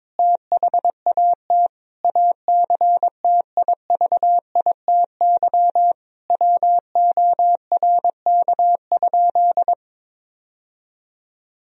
Morse code, heard as a keyed tone, about 22 wpm, 700 Hz. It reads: THAT ACTIVITY WORK?
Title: Sentences from Top 1000 Words - Review Part 4 - 22wpm